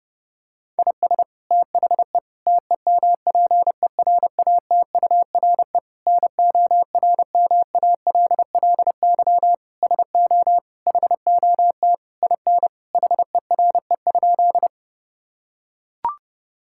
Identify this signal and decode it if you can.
Morse code, heard as a keyed tone, about 30 wpm, 700 Hz. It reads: IS THE TEMPERATURE NORMALLY SO HOT IN HERE?
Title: Sentences from Top 1000 Words - Review Part 6 - 30wpm